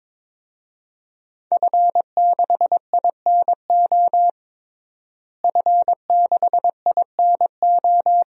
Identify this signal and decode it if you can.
Morse code, heard as a keyed tone, about 23 wpm, 700 Hz. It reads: F6INO F6INO
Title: DXCC-100 Call Signs - 5 Characters - Repeated 2x - 22wpm